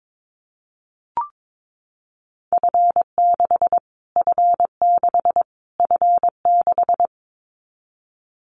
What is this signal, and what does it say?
Morse code, heard as a keyed tone, about 22 wpm, 700 Hz. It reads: F6 F6 F6
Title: Callsign Prefixes for Top 25 DXCC - Repeated 3x - 22wpm